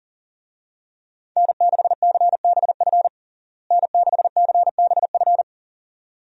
Morse code, heard as a keyed tone, about 40 wpm, 700 Hz. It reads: N6CBF N6CBF
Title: USA Call Signs - 5 Characters - Repeated 2x - 40wpm